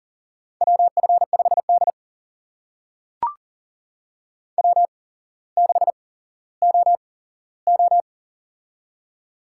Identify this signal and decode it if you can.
Morse code, heard as a keyed tone, about 40 wpm, spaced as for 17 wpm, 700 Hz. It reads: WF5D W6OO
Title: USA Call Signs - 4 Characters - 8x Character Spacing - 40wpm